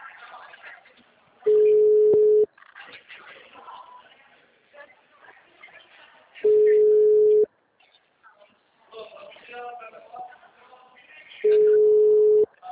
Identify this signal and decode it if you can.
Ringback tone (Europe, 425 Hz), with speech.